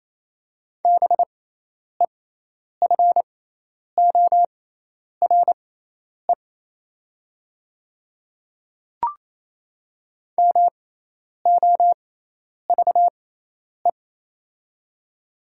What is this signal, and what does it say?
Morse code, heard as a keyed tone, about 28 wpm, spaced as for 10 wpm, 700 Hz. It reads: BEFORE MOVE
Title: 6x Character Spacing - Top 200 Words - 28wpm